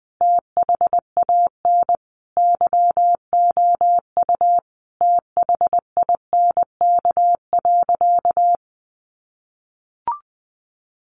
Morse code, heard as a keyed tone, about 20 wpm, 700 Hz. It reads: THAN YOU THINK.
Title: Sentences from Top 400 Words 20wpm